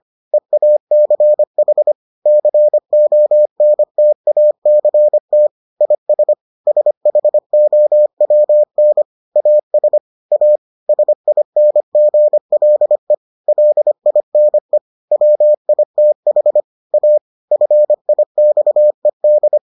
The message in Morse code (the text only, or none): EACH CONTACT IS SHOWN AS A SINGLE LINE WITH A FIXED